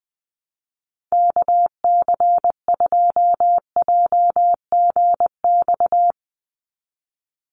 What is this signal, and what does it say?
Morse code, heard as a keyed tone, about 20 wpm, 700 Hz. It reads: KC2JGX
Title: USA Call Signs 20wpm